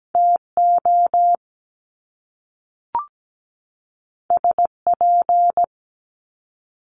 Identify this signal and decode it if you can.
Morse code, heard as a keyed tone, about 17 wpm, 700 Hz. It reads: TO SP
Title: Top Two Letter Combinations in Words 17wpm